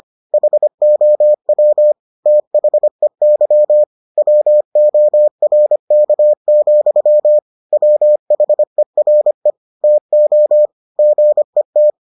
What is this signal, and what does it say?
Morse code, heard as a keyed tone, about 25 wpm, 600 Hz. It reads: HOW THEY WORK, WHERE TO GET